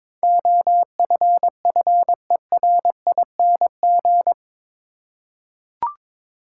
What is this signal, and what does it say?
Morse code, heard as a keyed tone, about 22 wpm, 700 Hz. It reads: OFFERING
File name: practice / 22wpm / Eight Letter Words - 22wpm